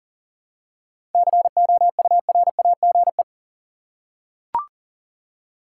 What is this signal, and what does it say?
Morse code, heard as a keyed tone, about 40 wpm, 700 Hz. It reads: COURAGE
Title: Seven Letter Words - 40wpm